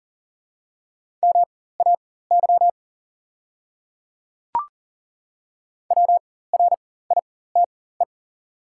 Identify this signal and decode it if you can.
Morse code, heard as a keyed tone, about 40 wpm, spaced as for 19 wpm, 700 Hz. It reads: MAY WRITE
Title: 4x Character Spacing - Top 100 Words - 40wpm